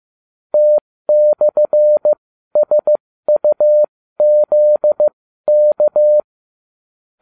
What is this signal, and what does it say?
Morse code, heard as a keyed tone, about 15 wpm, spaced as for 13 wpm, 600 Hz. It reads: T/SUZK